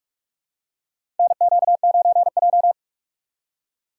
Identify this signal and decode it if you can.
Morse code, heard as a keyed tone, about 42 wpm, 700 Hz. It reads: NQ9J
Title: USA Call Signs - 6x Character Spacing - 45wpm